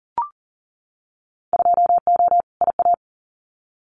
Morse code, heard as a keyed tone, about 41 wpm, 700 Hz. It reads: 2O IA
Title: Winter Field Day Exchanges 40wpm